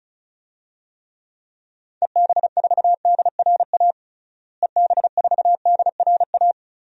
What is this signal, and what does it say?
Morse code, heard as a keyed tone, about 34 wpm, 700 Hz. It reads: EB4DRA EB4DRA
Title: DXCC-100 Call Signs - 6 Characters - Repeated 3x - 35wpm